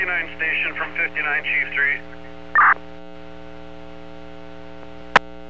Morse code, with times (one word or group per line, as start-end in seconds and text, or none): none